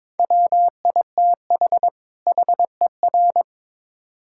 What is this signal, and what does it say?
Morse code, heard as a keyed tone, about 22 wpm, 700 Hz. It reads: WITH HER
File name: Build A Phrase - 2-Word Phrases - 22wpm